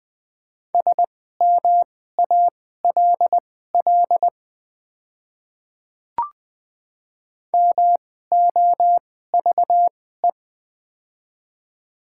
Morse code, heard as a keyed tone, about 20 wpm, spaced as for 15 wpm, 700 Hz. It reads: SMALL MOVE
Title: 2x Character Spacing - Top 300 Words - 20wpm